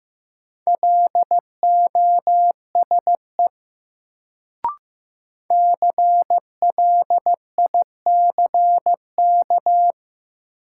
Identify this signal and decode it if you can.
Morse code, heard as a keyed tone, about 15 wpm, 700 Hz. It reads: LOSE CLICK
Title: Sentences from Top 1100 Words - New Word Drill Alt - 15wpm